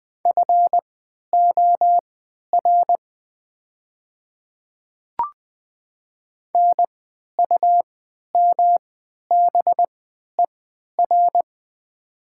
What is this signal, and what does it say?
Morse code, heard as a keyed tone, about 20 wpm, spaced as for 11 wpm, 700 Hz. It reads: FOR NUMBER
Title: 3x Character Spacing - Top 100 Words - 20wpm